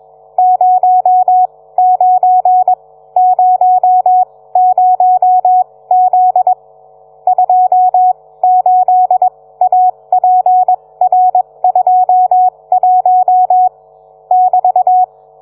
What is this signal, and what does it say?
Morse code, heard as a keyed tone, about 21 wpm, spaced as for 17 wpm, 740 Hz. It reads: 09 00Z 28APR21 <BT>